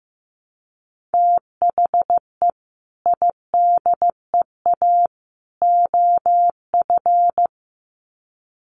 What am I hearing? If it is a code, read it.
Morse code, heard as a keyed tone, about 15 wpm, 700 Hz. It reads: THE IDEA OF
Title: Sets of 3 Words 15wpm